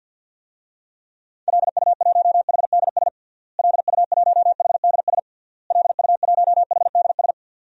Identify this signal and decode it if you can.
Morse code, heard as a keyed tone, about 52 wpm, 700 Hz. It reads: LU1HDS LU1HDS LU1HDS
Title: DXCC-100 Call Signs - 6 Characters - Repeated 3x - 50wpm